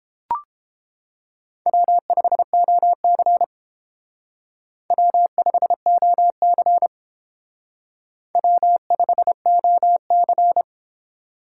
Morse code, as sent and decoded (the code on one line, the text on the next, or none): .-- ..... --- -.-. / .-- ..... --- -.-. / .-- ..... --- -.-.
W5OC W5OC W5OC